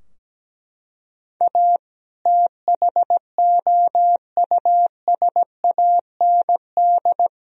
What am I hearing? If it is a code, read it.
Morse code, heard as a keyed tone, about 17 wpm, 700 Hz. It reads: A THOUSAND
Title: Sentences from Top 1000 Words - Review Part 3 - 17wpm